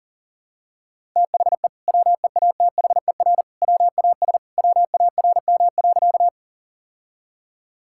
Morse code, heard as a keyed tone, about 40 wpm, 700 Hz. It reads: THE WEATHER WAS WARM.
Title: Sentences from Top 1000 Words - Review Part 5 - 40wpm